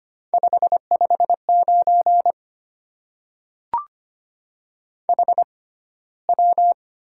Morse code, heard as a keyed tone, about 25 wpm, spaced as for 15 wpm, 700 Hz. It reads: 559 HW
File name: Top 90 QSO Elements - 6x Character Spacing - 25wpm